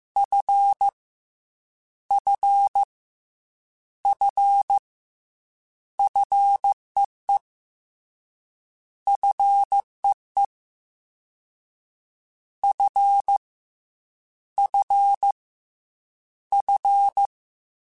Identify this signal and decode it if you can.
Morse code, heard as a keyed tone, about 15 wpm, 800 Hz. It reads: F F F FEE FEE F F F
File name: F and L / F and L mp3 / F and L Prefixes 3x2 Twice @13